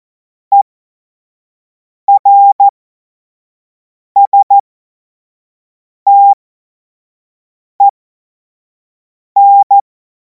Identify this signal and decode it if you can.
Morse code, heard as a keyed tone, about 14 wpm, spaced as for 5 wpm, 800 Hz. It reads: ERSTEN
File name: Morsekurs / Witze / Witz 31. März 2026 25 F70